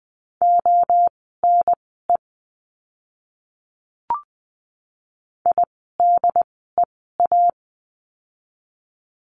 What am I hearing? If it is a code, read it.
Morse code, heard as a keyed tone, about 20 wpm, spaced as for 14 wpm, 700 Hz. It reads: ONE IDEA
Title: 108 Lesson 18f - Letter m - Words 20wpm